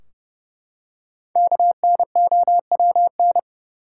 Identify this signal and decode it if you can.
Morse code, heard as a keyed tone, about 31 wpm, 700 Hz. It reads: KNOWN
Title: Five Letter Words - 30wpm